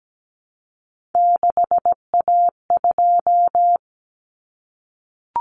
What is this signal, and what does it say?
Morse code, heard as a keyed tone, about 17 wpm, 700 Hz. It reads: 6A2
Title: Common Cut Numbers - 3-Digit 17wpm